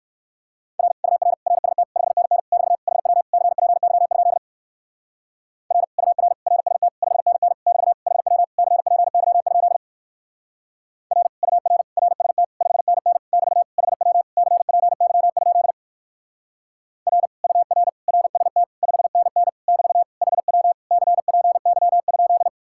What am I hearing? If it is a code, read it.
Morse code, heard as a keyed tone, about 58 wpm, 700 Hz. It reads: R UR RST 5NN <BT> HW CPY? R UR RST 5NN <BT> HW CPY? R UR RST 5NN <BT> HW CPY? R UR RST 5NN <BT> HW CPY?